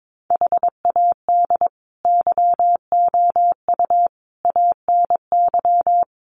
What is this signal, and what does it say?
Morse code, heard as a keyed tone, about 22 wpm, 700 Hz. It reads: HAD YOU ANY
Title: Sentences from Top 1500 Words 22wpm